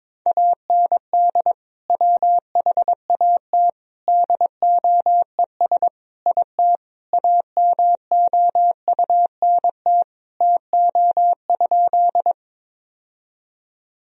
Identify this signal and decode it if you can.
Morse code, heard as a keyed tone, about 22 wpm, 700 Hz. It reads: AND WHAT DOES IT AMOUNT TO?
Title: Sentences from Top 1000 Words - Review Part 4 - 22wpm